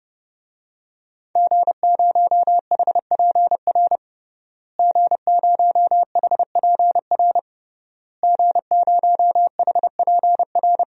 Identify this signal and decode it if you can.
Morse code, heard as a keyed tone, about 30 wpm, 700 Hz. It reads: G0HPR G0HPR G0HPR
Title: DXCC-100 Call Signs - 5 Characters - Repeated 3x - 30wpm